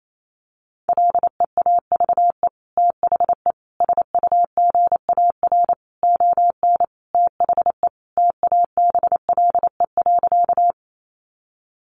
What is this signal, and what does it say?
Morse code, heard as a keyed tone, about 28 wpm, 700 Hz. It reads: LEAVE THE SUGAR ON THE TABLE.
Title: Sentences from Top 1200 Words 28wpm